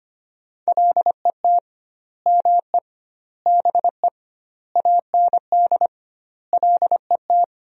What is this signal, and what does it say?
Morse code, heard as a keyed tone, about 25 wpm, 700 Hz. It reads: LET ME BE AND LET
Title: 2x Word-Spacing - Sentences from Top 300 Words - 25wpm